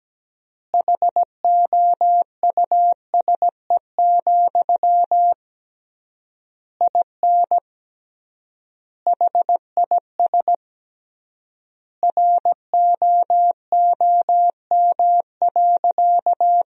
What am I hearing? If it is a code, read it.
Morse code, heard as a keyed tone, about 17 wpm, 700 Hz. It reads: HOUSE, IN HIS ROOM.
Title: 3x Word-Spacing - Sentences from Top 300 Words - 17wpm